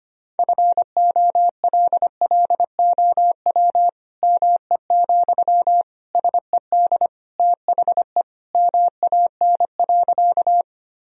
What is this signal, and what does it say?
Morse code, heard as a keyed tone, about 25 wpm, 700 Hz. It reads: FOLLOW ME, SED THE MAN.
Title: Sentences from Top 200 Words and Abbreviations 25wpm